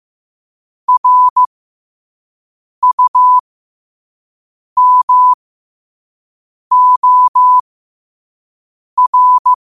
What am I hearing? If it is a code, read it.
Morse code, heard as a keyed tone, about 15 wpm, spaced as for 5 wpm, 1000 Hz. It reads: RUMOR
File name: qotdfiles / 5wpm